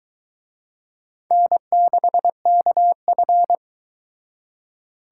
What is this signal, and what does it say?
Morse code, heard as a keyed tone, about 23 wpm, 700 Hz. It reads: N6KF